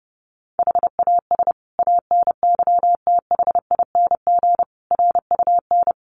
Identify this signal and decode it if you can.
Morse code, heard as a keyed tone, about 30 wpm, 700 Hz. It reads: HAS ANYTHING RUN
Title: Sentences from Top 1400 Words 30wpm